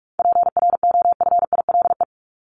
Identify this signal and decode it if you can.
Morse code, heard as a keyed tone, about 44 wpm, 700 Hz. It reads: PROFILE
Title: Top 1501-1750 Words 45wpm